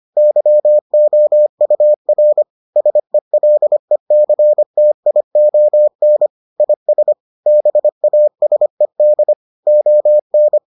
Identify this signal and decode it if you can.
Morse code, heard as a keyed tone, about 25 wpm, 600 Hz. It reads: YOUR SELECTION IS BASED ON